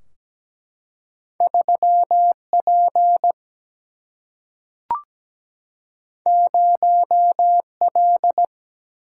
Morse code, heard as a keyed tone, about 17 wpm, 700 Hz. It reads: 3 P 0 L
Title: Single-Letter Single-Number 17wpm